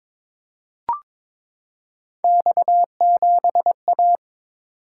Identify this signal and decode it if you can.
Morse code, heard as a keyed tone, about 23 wpm, 700 Hz. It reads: X7A
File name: practice / 22wpm / Three Letter-Number Combinations - 22wpm